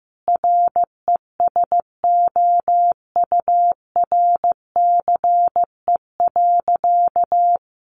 Morse code, heard as a keyed tone, about 15 wpm, 700 Hz. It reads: RESOURCE.
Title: Sentences from Top 1300 Words 15wpm